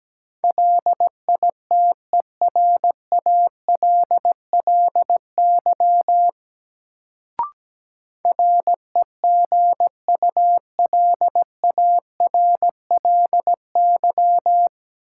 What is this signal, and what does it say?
Morse code, heard as a keyed tone, about 17 wpm, 700 Hz. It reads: LITERALLY REGULARLY
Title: Sentences from Top 1400 Words - New Word Drill Alt - 17wpm